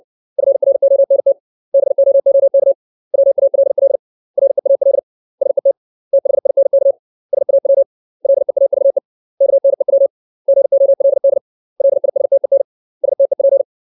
Morse code, heard as a keyed tone, about 60 wpm, 550 Hz. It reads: FRONT BOOK WALL REAL SET THINK SAW LEAVE KNEW GOLD LISTEN STEP
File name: 632 words / 632 Words 60WPM